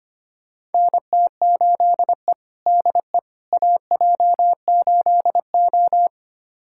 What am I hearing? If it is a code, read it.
Morse code, heard as a keyed tone, about 25 wpm, 700 Hz. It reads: NT8E DE AJ8O
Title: USA Callsign DE Callsign - 4 Characters - 25wpm